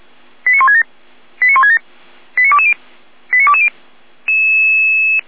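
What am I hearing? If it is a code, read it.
Five-tone selective call (ZVEI), heard as quick runs of five tones, tones of about 70 ms. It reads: 89167 89167 89220 89220
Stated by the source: ZVEI five-tone selective call received on SDR.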